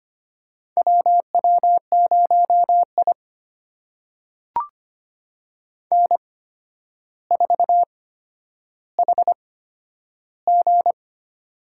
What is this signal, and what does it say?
Morse code, heard as a keyed tone, about 25 wpm, spaced as for 11 wpm, 700 Hz. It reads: WW0I N4HG
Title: USA Call Signs - 4 Characters - 8x Character Spacing - 25wpm